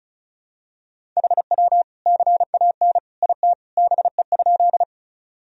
Morse code, heard as a keyed tone, about 36 wpm, 700 Hz. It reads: HW CAN IT BE?
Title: Sentences from Top 100 Words and Abbreviations 35wpm